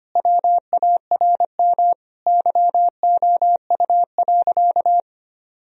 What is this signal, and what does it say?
Morse code, heard as a keyed tone, about 25 wpm, 700 Hz. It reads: WARM YOU.
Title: Sentences from Top 1000 Words - Review Part 2 - 25wpm